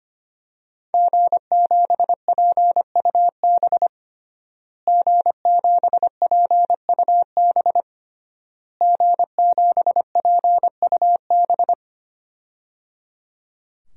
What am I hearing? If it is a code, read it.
Morse code, heard as a keyed tone, about 25 wpm, 700 Hz. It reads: G7PUB G7PUB G7PUB
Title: DXCC-100 Call Signs - 5 Characters - Repeated 3x - 25wpm